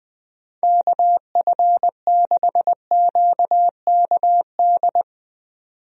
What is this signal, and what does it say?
Morse code, heard as a keyed tone, about 21 wpm, 700 Hz. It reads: KF6QKD
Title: USA Call Signs - 6 Characters - 20wpm